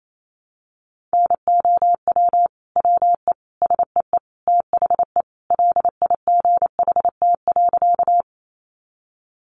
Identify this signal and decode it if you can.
Morse code, heard as a keyed tone, about 28 wpm, 700 Hz. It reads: NOW WE SEE THE LIGHT.
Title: Sentences from Top 200 Words 28wpm